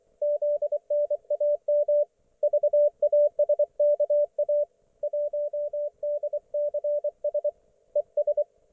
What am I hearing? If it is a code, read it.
Morse code, heard as a keyed tone, about 24 wpm, 590 Hz, with radio noise. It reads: ZNAM VASKA 1DCS ES
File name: Archiv / CUC212 / ok1fhi cul212